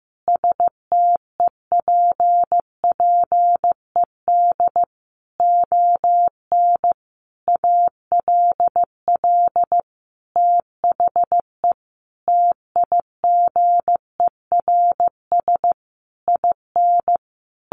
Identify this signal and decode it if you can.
Morse code, heard as a keyed tone, about 15 wpm, 700 Hz. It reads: STEPPED ON ALL THE TIGERS IN
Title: A Sound Of Thunder 15wpm